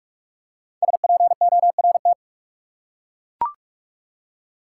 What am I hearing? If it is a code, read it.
Morse code, heard as a keyed tone, about 42 wpm, 700 Hz. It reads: SPORT
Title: Five Letter Words - 45wpm